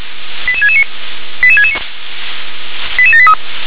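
Five-tone selective call (ZVEI), heard as quick runs of five tones, tones of about 70 ms. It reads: 88669 88669 88783
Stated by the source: ZVEI five-tone selective call 88650 received by SDR.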